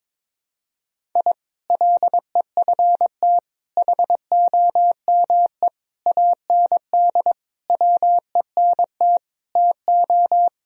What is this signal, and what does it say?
Morse code, heard as a keyed tone, about 22 wpm, 700 Hz. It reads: I LEFT HOME AND WENT TO
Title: Sentences from Top 1000 Words - Review Part 3 - 22wpm